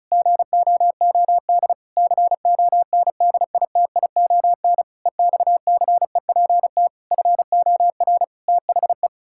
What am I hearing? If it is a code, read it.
Morse code, heard as a keyed tone, about 36 wpm, 700 Hz. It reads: GOOD CONDITION EXCEPT FOR THE
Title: Pangram Challenge v2 - 35wpm